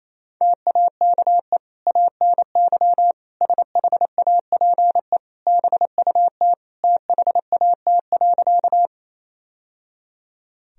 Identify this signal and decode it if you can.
Morse code, heard as a keyed tone, about 28 wpm, 700 Hz. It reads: TAKE ANY SHAPE BUT THAT.
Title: Sentences from Top 1000 Words - Review Part 3 - 28wpm